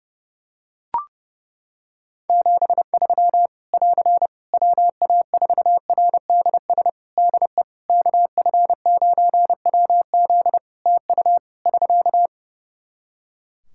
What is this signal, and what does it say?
Morse code, heard as a keyed tone, about 32 wpm, 700 Hz. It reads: 73 <AR> WA4RDS DE KF9WZ TU <SK>
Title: K1USN SST QSO Practice 30wpm